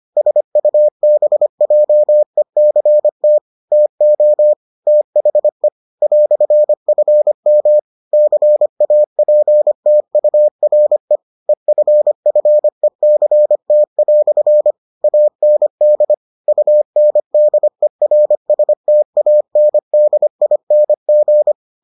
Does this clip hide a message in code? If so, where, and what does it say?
Yes: Morse code from 0.2 to 21.5 s, heard as a keyed tone, about 25 wpm, 600 Hz. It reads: SUBJECT TO THE "FM CAPTURE EFFECT" AND UNDERSTANDING